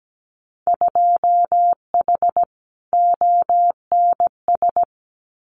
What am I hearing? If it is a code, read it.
Morse code, heard as a keyed tone, about 17 wpm, 700 Hz. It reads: 2H ONS
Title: Winter Field Day Exchanges Encore 17wpm